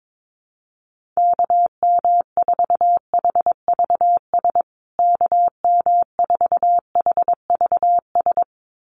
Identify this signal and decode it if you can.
Morse code, heard as a keyed tone, about 22 wpm, 700 Hz. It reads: KM4HVS KM4HVS
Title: USA Call Signs Repeated 2x 22wpm